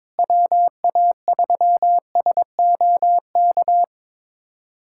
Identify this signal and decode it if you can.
Morse code, heard as a keyed tone, about 22 wpm, 700 Hz. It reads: WA3SOK
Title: USA Callsign DE Callsign 22wpm